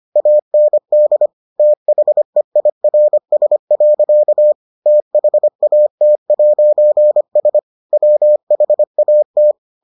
Morse code, heard as a keyed tone, about 25 wpm, 600 Hz. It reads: AND THEIRS. THAT'S WHAT